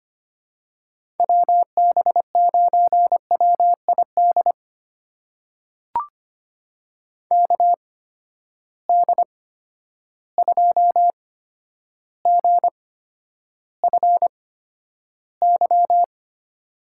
Morse code, heard as a keyed tone, about 25 wpm, spaced as for 11 wpm, 700 Hz. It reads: WB9WID KD2GFY
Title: USA Call Signs - 6 Characters - 8x Character Spacing - 25wpm